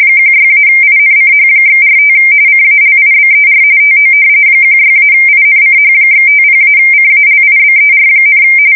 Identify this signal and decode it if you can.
RTTY, about 45.45 baud. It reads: 16-INCH/45 CALIBER TO 16-INCH/50-CALIBER,
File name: fsk170 / battleshipnj